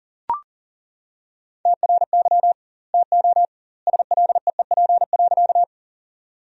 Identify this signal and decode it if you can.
Morse code, heard as a keyed tone, about 40 wpm, 700 Hz. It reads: TRY TO SLEEP.
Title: 2x Word-Spacing - Sentences from Top 500 Words - 40wpm